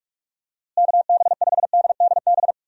Morse code, heard as a keyed tone, about 43 wpm, 700 Hz. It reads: KB5DDB
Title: USA Call Signs - 6 Characters - Repeated 2x - 45wpm